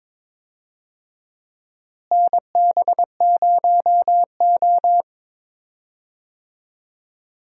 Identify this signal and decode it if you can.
Morse code, heard as a keyed tone, about 22 wpm, 700 Hz. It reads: NB0O
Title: USA Call Signs - 4 Characters - Repeated 2x - 22wpm